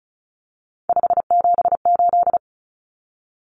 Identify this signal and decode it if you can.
Morse code, heard as a keyed tone, about 34 wpm, 700 Hz. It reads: 578
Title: Numbers - 3-Digit 35wpm